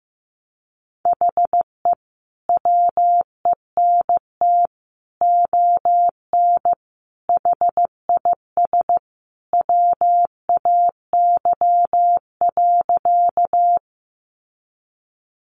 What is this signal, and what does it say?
Morse code, heard as a keyed tone, about 15 wpm, 700 Hz. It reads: HE WENT ON HIS WAY.
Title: Sentences from Top 200 Words 15wpm